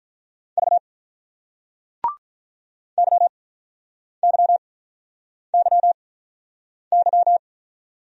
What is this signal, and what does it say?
Morse code, heard as a keyed tone, about 48 wpm, spaced as for 9 wpm, 700 Hz. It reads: V YYYY